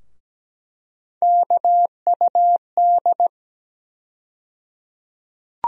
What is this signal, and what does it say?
Morse code, heard as a keyed tone, about 17 wpm, 700 Hz. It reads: KUD
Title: USA Call Sign Postfixes - 3 Letters 17wpm